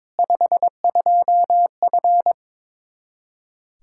Morse code, heard as a keyed tone, about 22 wpm, 700 Hz. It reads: 52F
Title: Practice QSO - Exchange 2 - Info - 22wpm